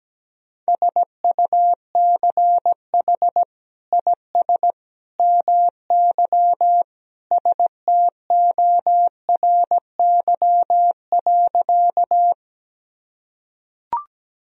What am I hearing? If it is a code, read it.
Morse code, heard as a keyed tone, about 17 wpm, 700 Hz. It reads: SUCH IS MY STORY.
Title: Sentences from Top 1000 Words - Review Part 1 - 17wpm